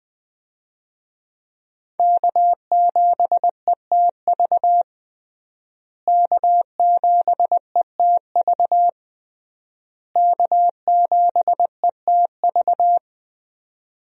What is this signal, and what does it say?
Morse code, heard as a keyed tone, about 20 wpm, 700 Hz. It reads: K7ETV K7ETV K7ETV
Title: USA Call Signs - 5 Characters - Repeated 3x - 20wpm